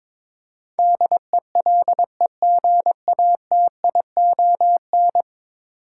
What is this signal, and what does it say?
Morse code, heard as a keyed tone, about 22 wpm, 700 Hz. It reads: DELEGATION
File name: practice / 22wpm / 10 Letter Words - First 3 Letters Given - 22wpm